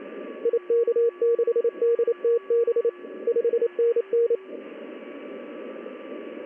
Morse code, heard as a keyed tone, about 28 wpm, 470 Hz. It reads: IK6DTB 5NN
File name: contests / ss-ssb-k5tr- -1700